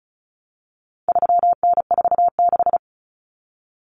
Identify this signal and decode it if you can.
Morse code, heard as a keyed tone, about 34 wpm, 700 Hz. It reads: 3N46
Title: Common Cut Numbers - 4-Digit 35wpm